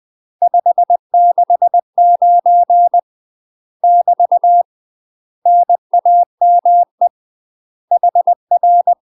Morse code, heard as a keyed tone, about 20 wpm, 700 Hz. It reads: 569 <BT> NAME HR
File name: UK MASTER QSOs TRAINING / 20WPM / UK MASTER QSOs 0073